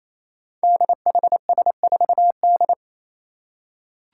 Morse code, heard as a keyed tone, about 27 wpm, 700 Hz. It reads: DHS4D
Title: Five Challenging Letter-Number Combinations 28wpm